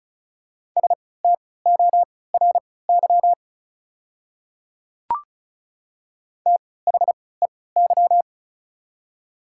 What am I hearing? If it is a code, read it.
Morse code, heard as a keyed tone, about 36 wpm, spaced as for 20 wpm, 700 Hz. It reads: STORY THEY
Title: 3x Character Spacing - Top 300 Words - 35wpm